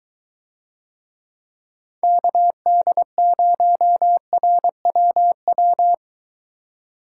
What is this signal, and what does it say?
Morse code, heard as a keyed tone, about 23 wpm, 700 Hz. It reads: KD0RWW